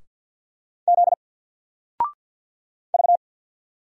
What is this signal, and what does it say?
Morse code, heard as a keyed tone, about 50 wpm, 700 Hz. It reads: Z V
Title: Single Letter-Number 50wpm